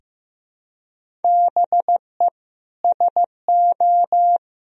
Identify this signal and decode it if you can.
Morse code, heard as a keyed tone, about 15 wpm, 700 Hz. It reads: BE SO
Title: Sentences from Top 1000 Words - Review Part 1 - 15wpm